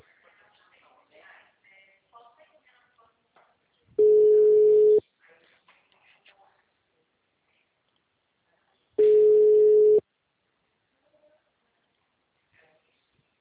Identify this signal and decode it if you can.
Ringback tone (Europe, 425 Hz), with speech.
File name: out- -2014- - - .27552